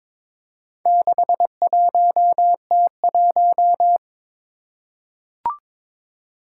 Morse code, heard as a keyed tone, about 23 wpm, 700 Hz. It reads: 61T1
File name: practice / 22wpm / Four Letter-Number Combinations - 22wpm